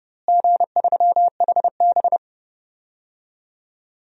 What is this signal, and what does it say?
Morse code, heard as a keyed tone, about 30 wpm, 700 Hz. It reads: G3HB
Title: DXCC-100 Call Signs - 4 Characters - Repeated 3x - 30wpm